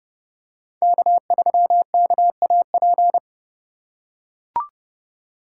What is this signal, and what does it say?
Morse code, heard as a keyed tone, about 30 wpm, 700 Hz. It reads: K3KAP
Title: USA Call Signs - 5 Characters - 4x Character Spacing - 30wpm